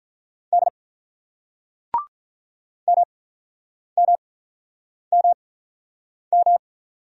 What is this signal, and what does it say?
Morse code, heard as a keyed tone, about 48 wpm, spaced as for 9 wpm, 700 Hz. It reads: D MMMM